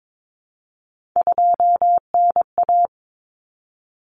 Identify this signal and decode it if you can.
Morse code, heard as a keyed tone, about 22 wpm, 700 Hz. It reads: 2NA
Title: Common Cut Numbers - 3-Digit 22wpm